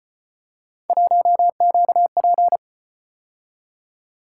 Morse code, heard as a keyed tone, about 36 wpm, 700 Hz. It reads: 1QP